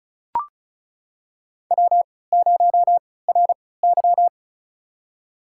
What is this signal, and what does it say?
Morse code, heard as a keyed tone, about 36 wpm, spaced as for 20 wpm, 700 Hz. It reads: W0RY
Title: USA Call Signs - 3x Character Spacing - 35wpm